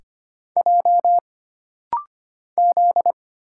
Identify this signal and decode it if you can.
Morse code, heard as a keyed tone, about 25 wpm, 700 Hz. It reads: J Z
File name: Single Character (Letter:Number:Punctuation:Prosign) - Rapid-Fire - 25wpm